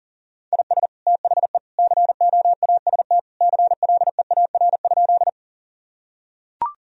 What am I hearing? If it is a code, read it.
Morse code, heard as a keyed tone, about 40 wpm, 700 Hz. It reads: IS THE COAST CLEAR?
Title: Sentences from Top 1000 Words - Review Part 4 - 40wpm